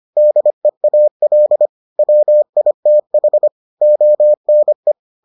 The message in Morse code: DEAL WITH ONE

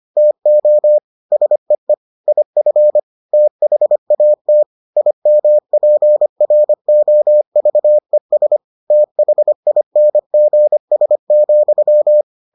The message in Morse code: TO SEE IF THAT IMPROVES THINGS,

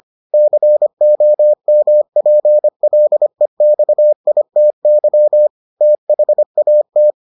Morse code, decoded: COMPLEXITY THAT

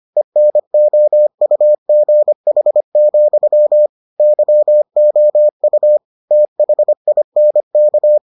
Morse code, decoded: ENOUGH, YOU THINK

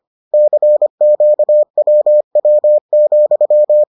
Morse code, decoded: CQWW,

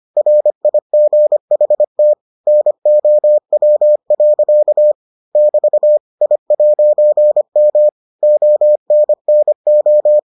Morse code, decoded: RIGHT NOW. <BT> I'M ONNO

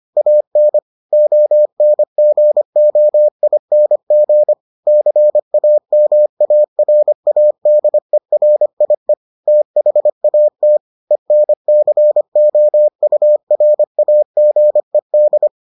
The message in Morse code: AN ONGOING CAMARADERIE THAT ENCOURAGED

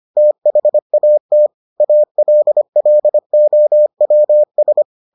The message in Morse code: THAT ALLOWS

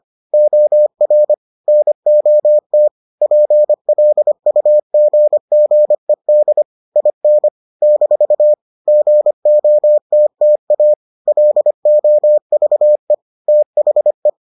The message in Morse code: OR NOT PLUGGED IN - GOTTA LOVE THE

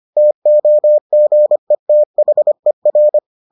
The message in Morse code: TOGETHER

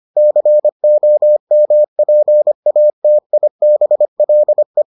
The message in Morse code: COMPATIBLE

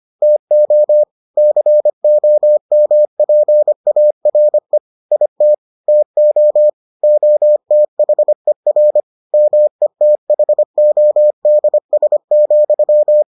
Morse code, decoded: TO COMPARE IT TO OTHER METHODS,

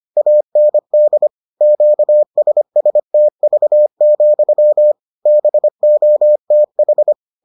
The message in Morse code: AND QSSTV, BOTH

